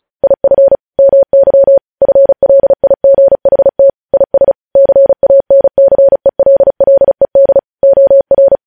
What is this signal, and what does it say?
Morse code, heard as a keyed tone, about 36 wpm, 550 Hz. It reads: IF MY FLIGHT IS CANCELLED OR
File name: podcast / News headlines-1x-35x35